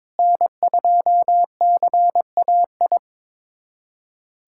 Morse code, heard as a keyed tone, about 21 wpm, 700 Hz. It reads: N2CAI
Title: USA Callsign DE Callsign - 5 Characters - 22wpm